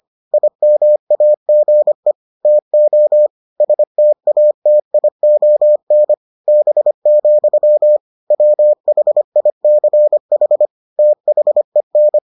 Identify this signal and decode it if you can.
Morse code, heard as a keyed tone, about 25 wpm, 600 Hz. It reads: IMAGE TO STATION B, WHICH THEN